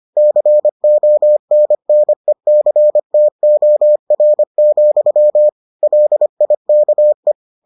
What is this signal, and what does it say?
Morse code, heard as a keyed tone, about 25 wpm, 600 Hz. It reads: CONNECTOR, LIKE